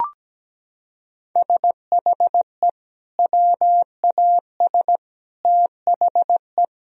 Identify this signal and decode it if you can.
Morse code, heard as a keyed tone, about 17 wpm, 700 Hz. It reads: SHE WAS THE
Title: Sentences from Top 1000 Words - Review Part 2 - 17wpm